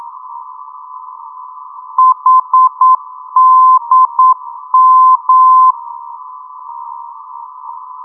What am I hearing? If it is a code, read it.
Morse code, heard as a keyed tone, about 9 wpm, 1030 Hz. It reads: HDM
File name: europe / d-8023 - ndb coleman hdm-334 (2004) (rainer brannolte, d) xc